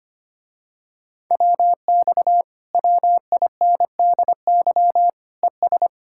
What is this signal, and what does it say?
Morse code, heard as a keyed tone, about 25 wpm, 700 Hz. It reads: WX WINDY ES
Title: Speed-Racing Common QSO Phrases - 25wpm